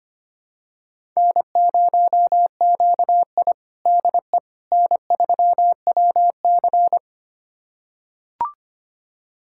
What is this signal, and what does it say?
Morse code, heard as a keyed tone, about 25 wpm, 700 Hz. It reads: N0QI DE N3WC
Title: USA Callsign DE Callsign - 4 Characters - 25wpm